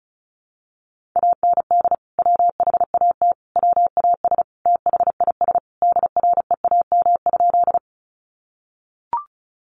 Morse code, heard as a keyed tone, about 34 wpm, 700 Hz. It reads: AND WHAT WAS THIS DREAM?
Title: Sentences from Top 1500 Words 35wpm